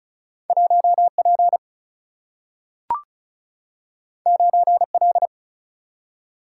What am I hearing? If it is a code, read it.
Morse code, heard as a keyed tone, about 36 wpm, 700 Hz. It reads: 1P 9L
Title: Single-Letter Single-Number 35wpm